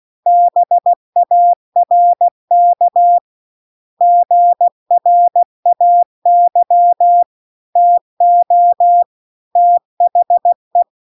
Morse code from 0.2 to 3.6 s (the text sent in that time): BARK